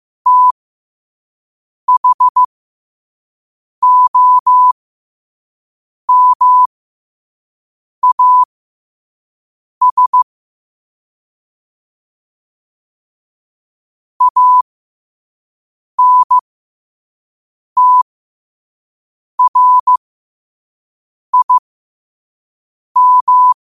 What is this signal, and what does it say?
Morse code, heard as a keyed tone, about 15 wpm, spaced as for 5 wpm, 1000 Hz. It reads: THOMAS ANTRIM